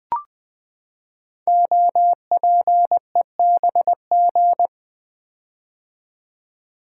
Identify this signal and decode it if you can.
Morse code, heard as a keyed tone, about 20 wpm, 700 Hz. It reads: OPEBG
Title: Code Groups - 5 Letter:Numbers - 20wpm